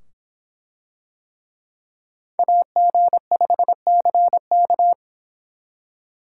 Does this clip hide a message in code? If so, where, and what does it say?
Yes: Morse code from 2.4 to 4.9 s, heard as a keyed tone, about 24 wpm, 700 Hz. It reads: AG5CK